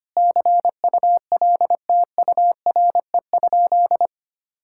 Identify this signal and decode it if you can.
Morse code, heard as a keyed tone, about 25 wpm, 700 Hz. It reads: CULTURE?
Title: Sentences from Top 1000 Words - Review Part 6 - 25wpm